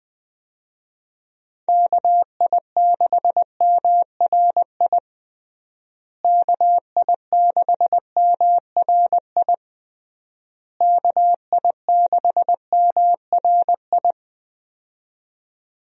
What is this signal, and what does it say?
Morse code, heard as a keyed tone, about 21 wpm, 700 Hz. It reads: KI6MRI KI6MRI KI6MRI
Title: USA Call Signs - 6 Characters - Repeated 3x - 20wpm